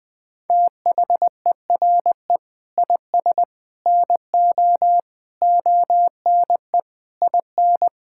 Morse code, heard as a keyed tone, about 20 wpm, 700 Hz. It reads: THERE IS NO ONE IN
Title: Sentences from Top 1000 Words - Review Part 6 - 20wpm